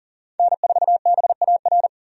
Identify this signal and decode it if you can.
Morse code, heard as a keyed tone, about 40 wpm, 700 Hz. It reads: N4BAR
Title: USA Call Signs - 5 Characters - 40wpm